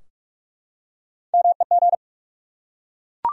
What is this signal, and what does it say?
Morse code, heard as a keyed tone, about 47 wpm, 700 Hz. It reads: MEG
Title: USA Common Names - Max 3 Letters - 45wpm